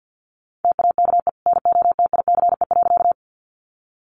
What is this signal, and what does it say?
Morse code, heard as a keyed tone, about 49 wpm, 700 Hz. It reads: TAKE NOTICE.
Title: Sentences from Top 400 Words 50wpm